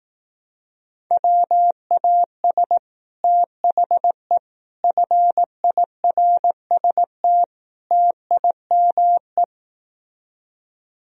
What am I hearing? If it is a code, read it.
Morse code, heard as a keyed tone, about 18 wpm, 700 Hz. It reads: WAS THE FIRST TIME
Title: Speed-Racing - Sets of 4 Words 15wpm